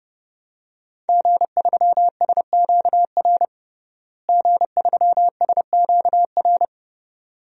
Morse code, heard as a keyed tone, about 30 wpm, 700 Hz. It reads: G3SQR G3SQR
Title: DXCC-100 Call Signs - 5 Characters - Repeated 2x - 30wpm